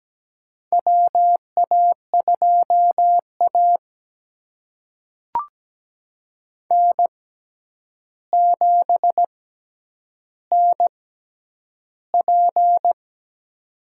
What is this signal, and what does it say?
Morse code, heard as a keyed tone, about 17 wpm, spaced as for 9 wpm, 700 Hz. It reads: WA2A N7NP
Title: USA Call Signs - 4 Characters - 6x Character Spacing - 17wpm